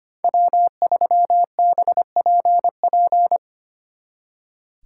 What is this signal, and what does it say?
Morse code, heard as a keyed tone, about 28 wpm, 700 Hz. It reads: W3BPP
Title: Speed Racing - USA Call Signs 25wpm